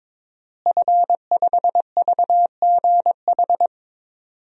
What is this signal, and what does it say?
Morse code, heard as a keyed tone, about 22 wpm, 700 Hz. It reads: F5VGH
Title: DXCC-100 Call Signs - 5 Characters - Repeated 3x - 22wpm